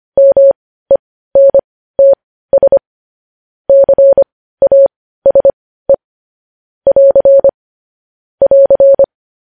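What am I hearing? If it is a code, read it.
Morse code, heard as a keyed tone, about 25 wpm, spaced as for 15 wpm, 550 Hz. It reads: MENTS CASE <AR> <AR>
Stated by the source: Morse code news-headline practice sent at 15 words per minute.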